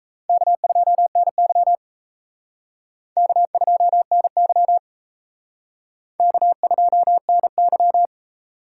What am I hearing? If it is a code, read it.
Morse code, heard as a keyed tone, about 34 wpm, 700 Hz. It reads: K2NY K2NY K2NY
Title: Speed Racing - USA 1x2 Call Signs 28wpm